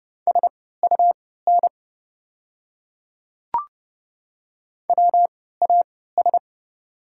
Morse code, heard as a keyed tone, about 30 wpm, spaced as for 17 wpm, 690 Hz. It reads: SUN WAS
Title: 3x Character Spacing - Top 300 Words - 30wpm